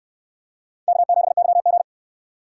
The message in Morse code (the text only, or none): K7YG